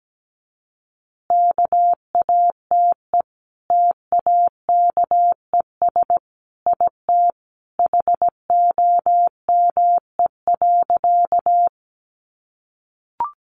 Morse code, heard as a keyed tone, about 17 wpm, 700 Hz. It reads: KATE TAKES IT HOME.
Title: Easy Phrases 17wpm